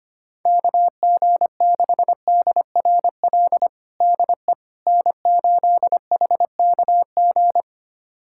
Morse code, heard as a keyed tone, about 25 wpm, 700 Hz. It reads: KG6DRL DE N8HKG
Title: USA Callsign DE Callsign 25wpm